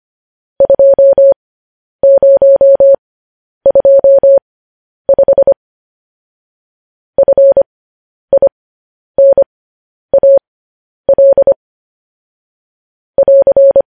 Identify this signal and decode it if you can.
Morse code, heard as a keyed tone, about 25 wpm, spaced as for 10 wpm, 550 Hz. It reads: 2025 FINAL <AR>
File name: podcast / News headlines-1x-25x10